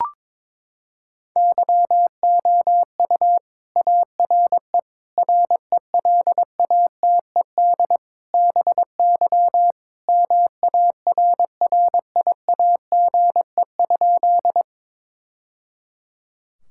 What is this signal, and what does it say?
Morse code, heard as a keyed tone, about 22 wpm, 700 Hz. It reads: YOU ARE RELATED BY MARRIAGE?
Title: Sentences from Top 1000 Words - Review Part 5 - 22wpm